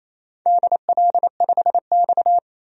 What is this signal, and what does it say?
Morse code, heard as a keyed tone, about 29 wpm, 700 Hz. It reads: DL5X
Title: DXCC-100 Call Signs - 4 Characters - 28wpm